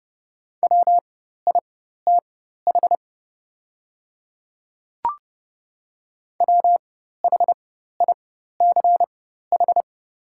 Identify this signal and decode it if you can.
Morse code, heard as a keyed tone, about 30 wpm, spaced as for 14 wpm, 700 Hz. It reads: WITH WHICH
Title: 4x Character Spacing - Top 100 Words - 30wpm